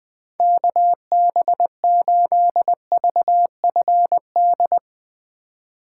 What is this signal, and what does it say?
Morse code, heard as a keyed tone, about 21 wpm, 700 Hz. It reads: KB8VFD